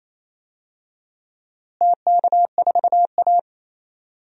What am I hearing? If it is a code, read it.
Morse code, heard as a keyed tone, about 28 wpm, 700 Hz. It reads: TK4A
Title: DXCC-100 Call Signs - 4 Characters - Repeated 2x - 28wpm